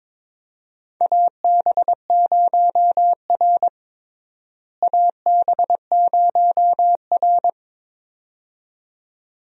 Morse code, heard as a keyed tone, about 22 wpm, 700 Hz. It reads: AB0R AB0R
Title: USA Call Signs - 4 Characters - Repeated 2x - 22wpm